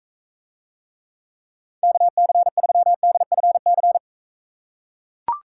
Morse code, heard as a keyed tone, about 43 wpm, 700 Hz. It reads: KC3DFC